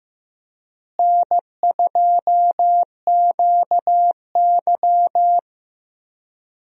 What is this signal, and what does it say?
Morse code, heard as a keyed tone, about 15 wpm, 700 Hz. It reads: N2QY
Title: USA Call Signs - 4 Characters - Repeated 2x - 15wpm